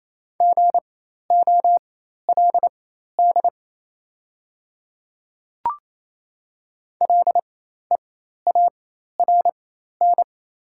Morse code, heard as a keyed tone, about 28 wpm, spaced as for 13 wpm, 700 Hz. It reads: GOLD LEARN
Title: 4x Character Spacing - Top 500 Words - 28wpm